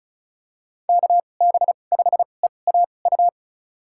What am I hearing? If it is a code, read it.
Morse code, heard as a keyed tone, about 34 wpm, spaced as for 25 wpm, 700 Hz. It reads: KB5EAU
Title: USA Call Signs - 6 Characters - 2x Character Spacing - 35wpm